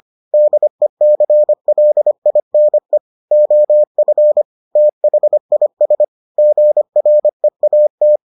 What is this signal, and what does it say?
Morse code, heard as a keyed tone, about 25 wpm, 600 Hz. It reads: DECLINE OF THIS GREAT